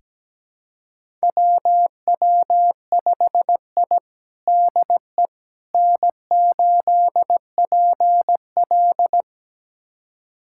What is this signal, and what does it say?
Morse code, heard as a keyed tone, about 17 wpm, 700 Hz. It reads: WW5I DE N8PL
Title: USA Callsign DE Callsign - 4 Characters - 17wpm